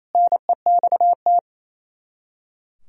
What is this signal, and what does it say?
Morse code, heard as a keyed tone, about 36 wpm, 700 Hz. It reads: NEXT